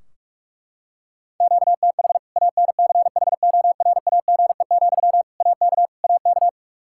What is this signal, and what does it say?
Morse code, heard as a keyed tone, about 46 wpm, 700 Hz. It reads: QTH ANCHORAGE, AK AK
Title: Practice QSO - Exchange 1 - Beginning - 45wpm